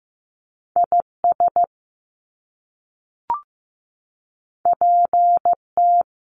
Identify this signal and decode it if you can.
Morse code, heard as a keyed tone, about 15 wpm, 700 Hz. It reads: IS PT
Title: Top Two Letter Combinations in Words 15wpm